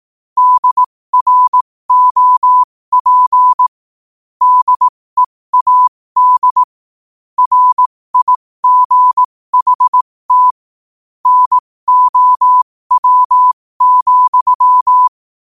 Morse code, heard as a keyed tone, about 18 wpm, spaced as for 15 wpm, 1000 Hz. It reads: DROP DEAD RIGHT NOW,
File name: qotdfiles / 15wpm